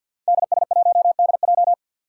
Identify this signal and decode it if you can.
Morse code, heard as a keyed tone, about 51 wpm, 700 Hz. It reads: DS1DJ
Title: Five Challenging Letter-Number Combinations 50wpm